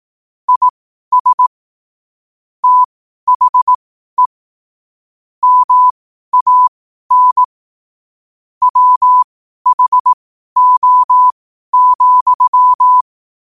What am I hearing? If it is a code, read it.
Morse code, heard as a keyed tone, about 18 wpm, spaced as for 12 wpm, 1000 Hz. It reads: IS THE MAN WHO,
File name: qotdfiles / 12wpm